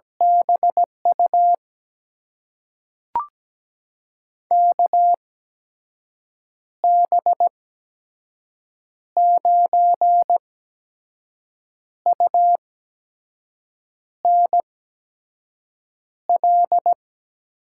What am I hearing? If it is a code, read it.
Morse code, heard as a keyed tone, about 17 wpm, spaced as for 5 wpm, 700 Hz. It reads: BU KB9UNL
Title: USA Call Signs - 6 Characters - 8x Character Spacing - 17wpm